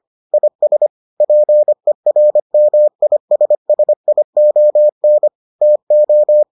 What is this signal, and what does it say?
Morse code, heard as a keyed tone, about 25 wpm, 600 Hz. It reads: IS PERMISSION TO